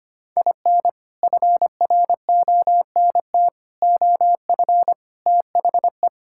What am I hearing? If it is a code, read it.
Morse code, heard as a keyed tone, about 25 wpm, 700 Hz. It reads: IN FRONT OF THE
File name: practice / 25wpm / Sentences with 4-Word Phrases - 25wpm